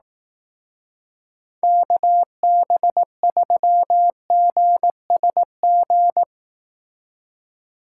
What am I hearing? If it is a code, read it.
Morse code, heard as a keyed tone, about 19 wpm, 700 Hz. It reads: KB3GSG
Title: Speed Racing - USA 2x3 Call Signs 15wpm